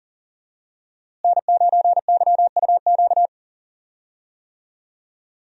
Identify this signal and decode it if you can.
Morse code, heard as a keyed tone, about 40 wpm, 700 Hz. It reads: N9YUQ
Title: USA Call Signs - 5 Characters - Repeated 2x - 40wpm